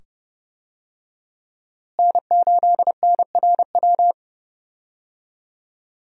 Morse code, heard as a keyed tone, about 30 wpm, 700 Hz. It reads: N8NRW
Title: USA Call Signs - 5 Characters - Repeated 2x - 30wpm